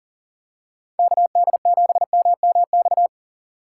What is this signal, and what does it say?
Morse code, heard as a keyed tone, about 38 wpm, 700 Hz. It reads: KD7MMX
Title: USA Call Signs - 6 Characters - 2x Character Spacing - 40wpm